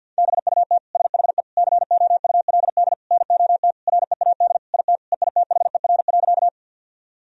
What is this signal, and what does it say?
Morse code, heard as a keyed tone, about 48 wpm, 700 Hz. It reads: BUT SHE COULD NOT READ IT EITHER.
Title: Sentences from Top 1000 Words - Review Part 4 - 50wpm